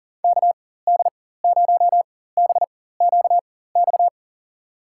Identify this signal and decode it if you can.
Morse code, heard as a keyed tone, about 37 wpm, spaced as for 18 wpm, 700 Hz. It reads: KD0BQX
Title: USA Call Signs - 4x Character Spacing - 40wpm